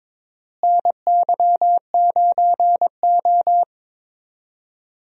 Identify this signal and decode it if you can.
Morse code, heard as a keyed tone, about 22 wpm, 700 Hz. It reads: NY9O
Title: USA Call Signs - 4 Characters - Repeated 3x - 22wpm